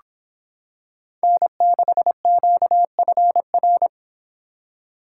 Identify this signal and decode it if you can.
Morse code, heard as a keyed tone, about 26 wpm, 700 Hz. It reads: N6QFR